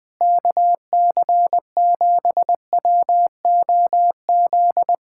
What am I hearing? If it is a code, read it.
Morse code, heard as a keyed tone, about 20 wpm, 700 Hz. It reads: KC7WOZ
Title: Speed-Racing Common QSO Phrases - 17wpm